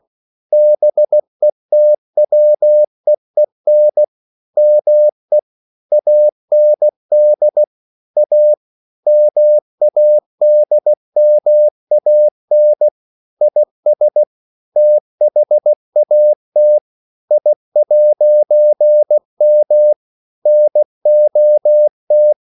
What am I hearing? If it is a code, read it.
Morse code, heard as a keyed tone, about 16 wpm, 600 Hz. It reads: BETWEEN ME AND A MADMAN IS THAT I'M NOT
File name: CW / Practice / EnglishQuotations0116